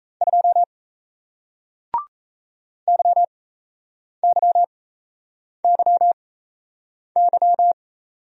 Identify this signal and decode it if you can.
Morse code, heard as a keyed tone, about 37 wpm, spaced as for 8 wpm, 700 Hz. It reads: 2 YYYY